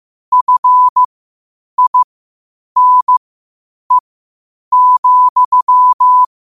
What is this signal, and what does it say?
Morse code, heard as a keyed tone, about 15 wpm, spaced as for 8 wpm, 1000 Hz. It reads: FINE,
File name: qotdfiles / 8wpm